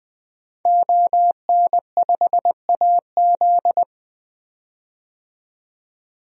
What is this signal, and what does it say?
Morse code, heard as a keyed tone, about 20 wpm, 700 Hz. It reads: ON5AZ
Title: DXCC-100 Call Signs - 5 Characters - 20wpm